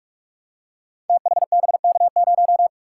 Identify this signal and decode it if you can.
Morse code, heard as a keyed tone, about 44 wpm, 700 Hz. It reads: THBK0
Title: Code Groups - 5 Letter:Numbers - 45wpm